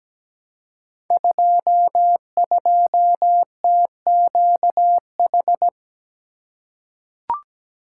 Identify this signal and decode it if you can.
Morse code, heard as a keyed tone, about 17 wpm, 700 Hz. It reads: 22TQH
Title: Code Groups - 5 Letter:Numbers - 17wpm